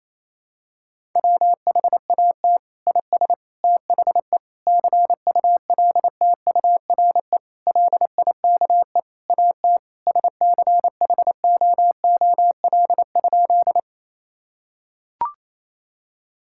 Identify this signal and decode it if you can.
Morse code, heard as a keyed tone, about 28 wpm, 700 Hz. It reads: WHAT IS THE CULTURE LIKE AT SCHOOL?
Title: Sentences from Top 1000 Words - Review Part 6 - 28wpm